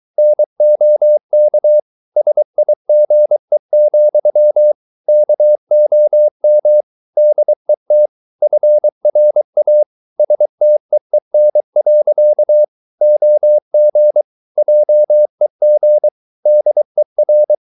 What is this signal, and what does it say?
Morse code, heard as a keyed tone, about 23 wpm, 600 Hz. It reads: NOK SIGE, KOM DET FRA STEEN. OG JEG DER
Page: Ferie 23 wpm